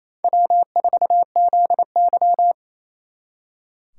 Morse code, heard as a keyed tone, about 30 wpm, 700 Hz. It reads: W4ZY
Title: Speed Racing - USA Call Signs 28wpm